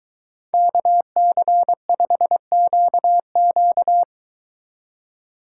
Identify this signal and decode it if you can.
Morse code, heard as a keyed tone, about 23 wpm, 700 Hz. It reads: KC5QQ